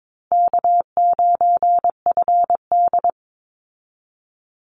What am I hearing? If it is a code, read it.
Morse code, heard as a keyed tone, about 22 wpm, 700 Hz. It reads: K9FD
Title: Call Signs Repeated 2x - 22wpm